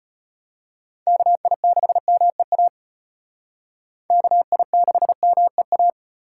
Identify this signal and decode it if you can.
Morse code, heard as a keyed tone, about 32 wpm, 700 Hz. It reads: KI6MEA KI6MEA